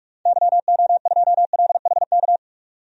Morse code, heard as a keyed tone, about 44 wpm, 700 Hz. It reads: YO2LHK
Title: DXCC-100 Call Signs - 6 Characters - 45wpm